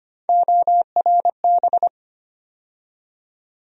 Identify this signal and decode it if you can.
Morse code, heard as a keyed tone, about 25 wpm, 700 Hz. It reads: ORB
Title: USA Call Sign Postfixes - 3 Letters 25wpm